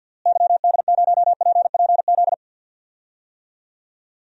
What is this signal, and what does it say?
Morse code, heard as a keyed tone, about 52 wpm, 700 Hz. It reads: YD0PPZ
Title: DXCC-100 Call Signs - 6 Characters - Repeated 3x - 50wpm